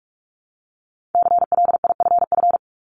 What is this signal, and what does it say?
Morse code, heard as a keyed tone, about 46 wpm, 700 Hz. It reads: CLIFF
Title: Multi-letter Phonemes Repeated 3x 45wpm